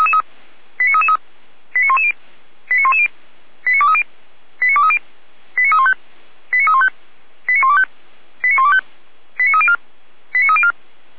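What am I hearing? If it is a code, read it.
Five-tone selective call (ZVEI), heard as quick runs of five tones, tones of about 70 ms. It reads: 89333 89110 89110 89230 89230 89315 89315 89125 89125 89304 89304
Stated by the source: ZVEI five-tone selective call received with an SDR.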